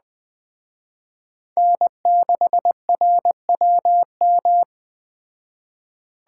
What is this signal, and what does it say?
Morse code, heard as a keyed tone, about 20 wpm, 700 Hz. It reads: N6RWM
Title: Speed Racing - USA 1x3 Call Signs 17wpm